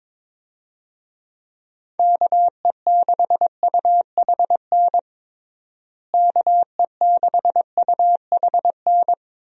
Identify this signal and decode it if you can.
Morse code, heard as a keyed tone, about 22 wpm, 700 Hz. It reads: KE6UHN KE6UHN
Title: USA Call Signs - 6 Characters - Repeated 3x - 22wpm